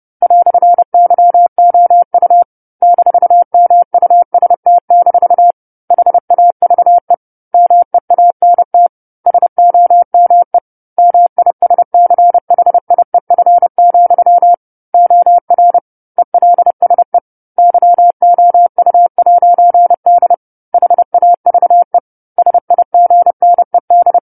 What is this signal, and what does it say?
Morse code, heard as a keyed tone, about 30 wpm, 700 Hz. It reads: "YOU -MUST- HAVE MEANT SOME MISCHIEF, OR ELSE YOU'D HAVE SIGNED
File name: 10-Chapter-10 30 WPM